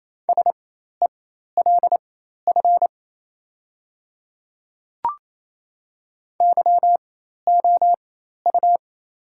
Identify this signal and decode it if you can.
Morse code, heard as a keyed tone, about 28 wpm, spaced as for 13 wpm, 700 Hz. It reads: SELF YOU
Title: 4x Character Spacing - Top 300 Words - 28wpm